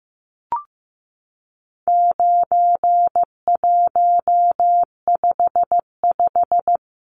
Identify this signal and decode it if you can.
Morse code, heard as a keyed tone, about 15 wpm, 700 Hz. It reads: 9155
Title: Numbers - 4-Digit 15wpm